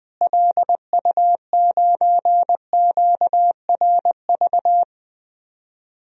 Morse code, heard as a keyed tone, about 21 wpm, 700 Hz. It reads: LU9QRV
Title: DXCC-100 Call Signs - 6 Characters - Repeated 2x - 20wpm